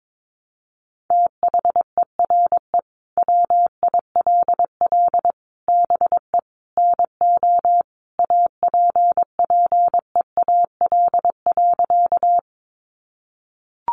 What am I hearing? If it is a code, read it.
Morse code, heard as a keyed tone, about 22 wpm, 700 Hz. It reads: THERE WILL BE NO APPEAL.
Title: Sentences from Top 1300 Words 22wpm